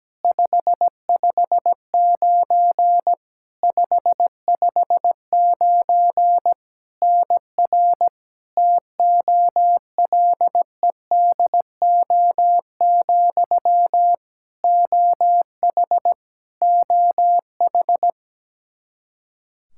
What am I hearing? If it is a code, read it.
Morse code, heard as a keyed tone, about 17 wpm, 700 Hz. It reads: 559 559 NR TOLEDO, OH OH
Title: Practice QSO - Exchange 1 - Beginning - 17wpm